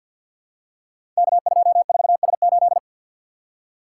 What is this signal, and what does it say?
Morse code, heard as a keyed tone, about 52 wpm, 700 Hz. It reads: K24S8
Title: Code Groups - 5 Letter:Numbers - 50wpm